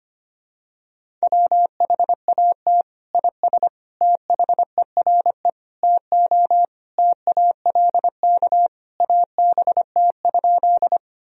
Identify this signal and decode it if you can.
Morse code, heard as a keyed tone, about 25 wpm, 700 Hz. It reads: WHAT IS THERE TO TALK ABT?
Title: Sentences from Top 400 Words and Abbreviations 25wpm